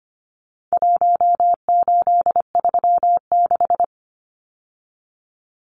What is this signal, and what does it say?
Morse code, heard as a keyed tone, about 25 wpm, 700 Hz. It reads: 1836
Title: Numbers - Contest Style - 4-Digit 25wpm